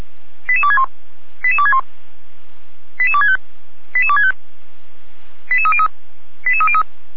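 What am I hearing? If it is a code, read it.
Five-tone selective call (ZVEI), heard as quick runs of five tones, tones of about 70 ms. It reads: 88261 88261 88265 88265 88303 88303
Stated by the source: ZVEI five-tone selective call received on SDR by feuerwehr-orlamuende.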